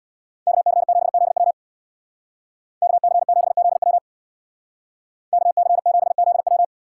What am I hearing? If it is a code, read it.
Morse code, heard as a keyed tone, about 64 wpm, 700 Hz. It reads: KC7ZW KC7ZW KC7ZW